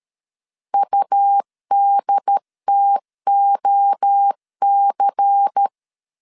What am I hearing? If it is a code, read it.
Morse code, heard as a keyed tone, about 13 wpm, 800 Hz. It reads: UDTOC
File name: CW / DET / 1 6 1500C S C 12wpm 3533